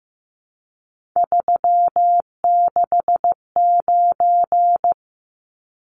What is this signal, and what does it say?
Morse code, heard as a keyed tone, about 15 wpm, 700 Hz. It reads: 369
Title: Numbers - 3-Digit 15wpm